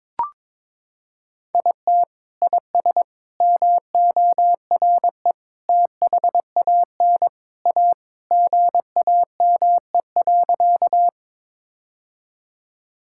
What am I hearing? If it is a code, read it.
Morse code, heard as a keyed tone, about 22 wpm, 700 Hz. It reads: IT IS MORE THAN A GAME.
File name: practice / 22wpm / Sentences with 3-Word Phrases - 22wpm